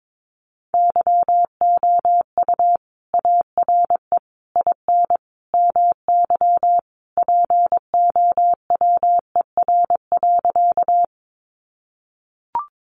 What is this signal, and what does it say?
Morse code, heard as a keyed tone, about 22 wpm, 700 Hz. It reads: YOU ARE IN MY POWER.
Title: Sentences from Top 500 Words 22wpm